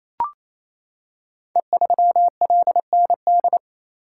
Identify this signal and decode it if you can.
Morse code, heard as a keyed tone, about 27 wpm, 700 Hz. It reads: E3LND
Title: Code Groups - 5 Letter:Numbers - 28wpm